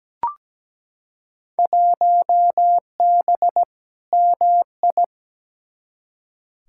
Morse code, heard as a keyed tone, about 17 wpm, 700 Hz. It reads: 1B MI
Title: Field Day Exchanges - 17wpm